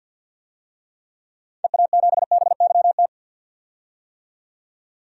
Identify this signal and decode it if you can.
Morse code, heard as a keyed tone, about 52 wpm, 700 Hz. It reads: EA7BYT
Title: DXCC-100 Call Signs - 6 Characters - 50wpm